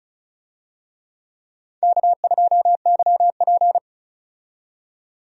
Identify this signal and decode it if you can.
Morse code, heard as a keyed tone, about 36 wpm, 700 Hz. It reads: K2YP
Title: USA Call Signs - 4 Characters - 35wpm